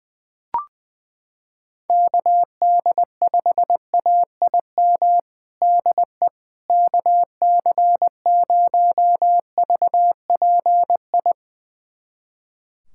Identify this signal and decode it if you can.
Morse code, heard as a keyed tone, about 20 wpm, 700 Hz. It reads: KD5AIM DE KC0VPI
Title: USA Callsign DE Callsign 20wpm